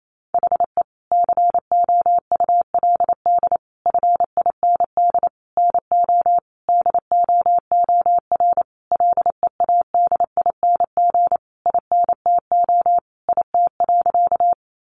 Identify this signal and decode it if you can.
Morse code, heard as a keyed tone, about 28 wpm, 700 Hz. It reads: HE COULD FIND NO DOOR LEADING INTO IT.
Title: Sentences from Top 1100 Words 28wpm